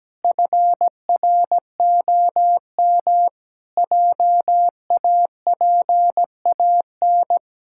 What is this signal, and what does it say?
Morse code, heard as a keyed tone, about 17 wpm, 700 Hz. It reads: FROM JAPAN
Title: Pangram Challenge v2 - 17wpm